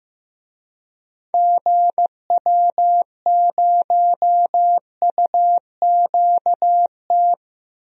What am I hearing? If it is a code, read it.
Morse code, heard as a keyed tone, about 15 wpm, 700 Hz. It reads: GW0UQT
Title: DXCC-100 Call Signs - 6 Characters - Repeated 2x - 15wpm